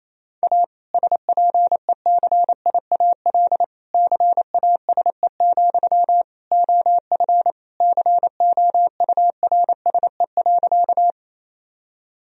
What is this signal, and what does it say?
Morse code, heard as a keyed tone, about 28 wpm, 700 Hz. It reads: A SPECIAL CASE, OF COURSE.
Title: Sentences from Top 1000 Words - Review Part 3 - 28wpm